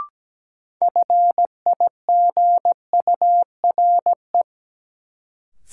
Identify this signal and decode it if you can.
Morse code, heard as a keyed tone, about 17 wpm, 700 Hz. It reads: FIGURE
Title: Sets of 4 Words - New Word Drill - 17wpm